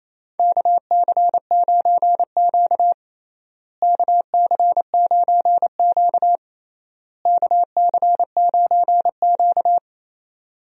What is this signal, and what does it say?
Morse code, heard as a keyed tone, about 27 wpm, 700 Hz. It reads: KC9Q KC9Q KC9Q
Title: USA Call Signs - 4 Characters - Repeated 3x - 28wpm